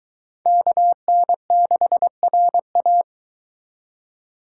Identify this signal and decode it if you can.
Morse code, heard as a keyed tone, about 22 wpm, 700 Hz. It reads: KN6RA